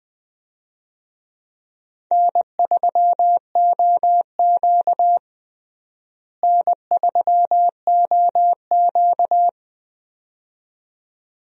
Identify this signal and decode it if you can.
Morse code, heard as a keyed tone, about 20 wpm, 700 Hz. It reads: N3OQ N3OQ
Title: USA Call Signs - 4 Characters - Repeated 2x - 20wpm